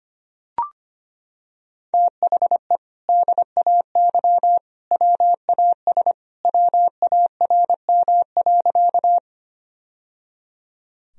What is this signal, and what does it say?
Morse code, heard as a keyed tone, about 25 wpm, 700 Hz. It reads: THE DAY WAS WARM.
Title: Sentences from Top 1000 Words - Review Part 2 - 25wpm